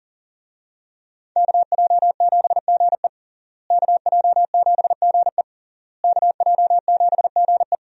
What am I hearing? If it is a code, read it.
Morse code, heard as a keyed tone, about 40 wpm, 700 Hz. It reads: KJ7GE KJ7GE KJ7GE
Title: USA Call Signs - 5 Characters - Repeated 3x - 40wpm